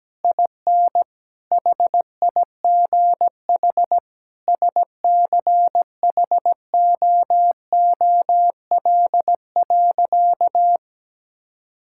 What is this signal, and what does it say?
Morse code, heard as a keyed tone, about 17 wpm, 700 Hz. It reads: IN HIGH SCHOOL.
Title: Sentences from Top 1000 Words - Review Part 6 - 17wpm